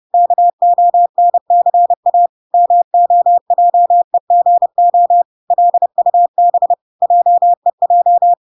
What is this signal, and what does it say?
Morse code, heard as a keyed tone, about 30 wpm, 700 Hz. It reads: KONCA MOJEGO LUB JEJ
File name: recordings / 30wpm0000